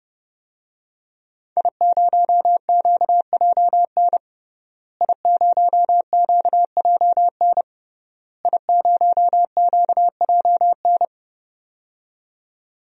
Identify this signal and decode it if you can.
Morse code, heard as a keyed tone, about 30 wpm, 700 Hz. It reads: I0QJN I0QJN I0QJN
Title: DXCC-100 Call Signs - 5 Characters - Repeated 3x - 30wpm